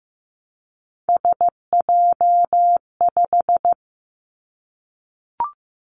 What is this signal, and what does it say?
Morse code, heard as a keyed tone, about 15 wpm, 700 Hz. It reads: SJ5
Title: Three Challenging Letter-Number Combinations 15wpm